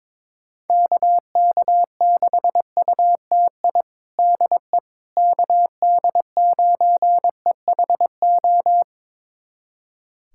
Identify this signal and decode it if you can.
Morse code, heard as a keyed tone, about 21 wpm, 700 Hz. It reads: KK6UTI DE KD9EHO
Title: USA Callsign DE Callsign 22wpm